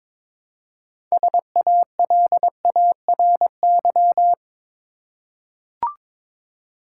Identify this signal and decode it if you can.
Morse code, heard as a keyed tone, about 22 wpm, 700 Hz. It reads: SALARY
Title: Six Letter Words - 22wpm